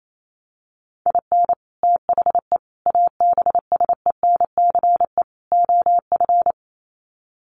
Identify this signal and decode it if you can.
Morse code, heard as a keyed tone, about 28 wpm, 700 Hz. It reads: IN THE ABSENCE OF
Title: Sets of 4 Words 28wpm